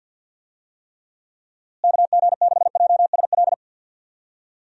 Morse code, heard as a keyed tone, about 50 wpm, 700 Hz. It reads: KG6JSL
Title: USA Call Signs - 6 Characters - Repeated 3x - 50wpm